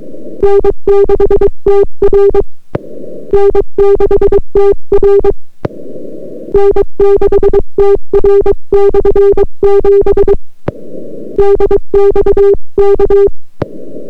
Morse code, heard as a keyed tone, about 22 wpm, 400 Hz. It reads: N6TR N6TR N6TR/7 DXK